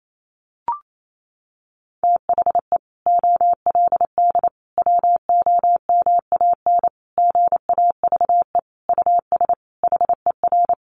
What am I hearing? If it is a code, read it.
Morse code, heard as a keyed tone, about 28 wpm, 700 Hz. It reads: THE OLD WOMAN GAVE US HER
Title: Sentences from Top 1100 Words 28wpm